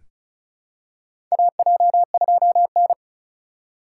Morse code, heard as a keyed tone, about 36 wpm, 700 Hz. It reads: AJ2N
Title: USA Call Signs - 4 Characters - 4x Character Spacing - 35wpm